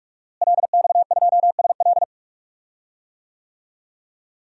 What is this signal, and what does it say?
Morse code, heard as a keyed tone, about 46 wpm, 700 Hz. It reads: LX2SL
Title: DXCC-100 Call Signs - 5 Characters - Repeated 3x - 45wpm